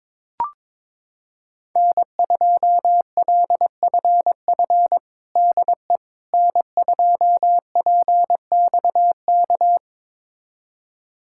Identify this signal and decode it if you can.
Morse code, heard as a keyed tone, about 22 wpm, 700 Hz. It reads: N2LFF DE N2PXK
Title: USA Callsign DE Callsign - 5 Characters - 22wpm